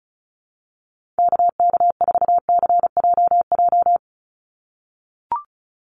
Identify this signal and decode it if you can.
Morse code, heard as a keyed tone, about 34 wpm, 700 Hz. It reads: KK4CJJ
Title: USA Call Signs 35wpm